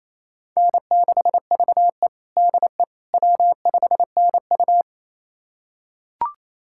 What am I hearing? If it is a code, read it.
Morse code, heard as a keyed tone, about 28 wpm, 700 Hz. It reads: N6VE DE W5NU
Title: USA Callsign DE Callsign - 4 Characters - 28wpm